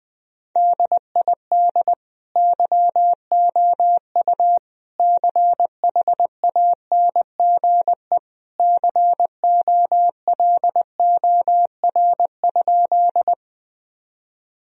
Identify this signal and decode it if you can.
Morse code, heard as a keyed tone, about 20 wpm, 700 Hz. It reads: DID YOU CHANGE COLOR?
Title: Sentences from Top 1000 Words - Review Part 2 - 20wpm